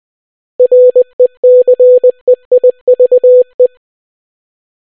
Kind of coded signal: Morse code